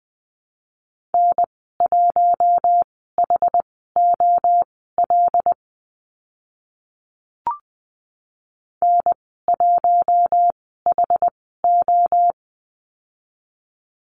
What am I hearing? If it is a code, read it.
Morse code, heard as a keyed tone, about 20 wpm, spaced as for 14 wpm, 700 Hz. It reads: N1HOL N1HO
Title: 088 Lesson 15g - Number 5 - Call Signs 20wpm